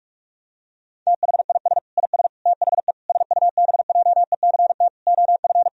Morse code, heard as a keyed tone, about 42 wpm, 700 Hz. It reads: THIS IS THE SUBJECT OF
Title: Sentences from Top 1000 Words - Review Part 4 - 45wpm